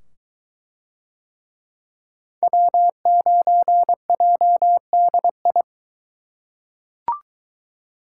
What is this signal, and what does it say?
Morse code, heard as a keyed tone, about 22 wpm, 700 Hz. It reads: W9JDI